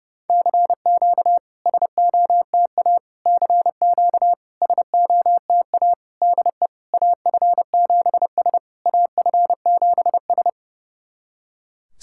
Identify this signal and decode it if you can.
Morse code, heard as a keyed tone, about 30 wpm, 700 Hz. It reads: CQ SOTA CQ SOTA DE AF7S AF7S
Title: SOTA QSO Practice 30wpm